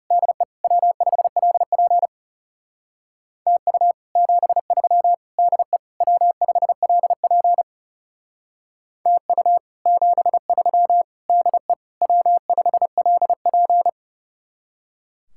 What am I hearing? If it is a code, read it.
Morse code, heard as a keyed tone, about 39 wpm, 700 Hz. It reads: DE W5LP TU 73 DE W5LP TU 73 DE W5LP